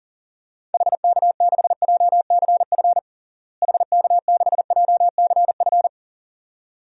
Morse code, heard as a keyed tone, about 40 wpm, 700 Hz. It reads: HK6JCF HK6JCF
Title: DXCC-100 Call Signs - 6 Characters - Repeated 2x - 40wpm